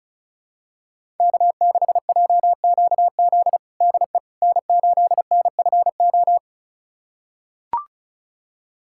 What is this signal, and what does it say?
Morse code, heard as a keyed tone, about 36 wpm, 690 Hz. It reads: K6JQZ DE N8NFO
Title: USA Callsign DE Callsign - 5 Characters - 35wpm